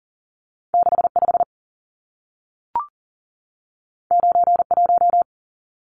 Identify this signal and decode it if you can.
Morse code, heard as a keyed tone, about 40 wpm, 700 Hz. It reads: 65 91
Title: Numbers - 2-Digit 40wpm